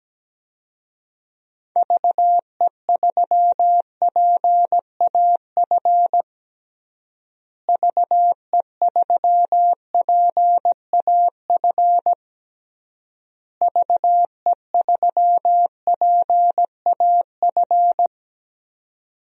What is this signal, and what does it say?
Morse code, heard as a keyed tone, about 17 wpm, 700 Hz. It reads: VE3PAF VE3PAF VE3PAF
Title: DXCC-100 Call Signs - 6 Characters - Repeated 3x - 17wpm